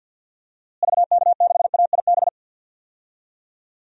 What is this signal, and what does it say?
Morse code, heard as a keyed tone, about 46 wpm, 700 Hz. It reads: VK6AIB
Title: DXCC-100 Call Signs - 6 Characters - Repeated 2x - 50wpm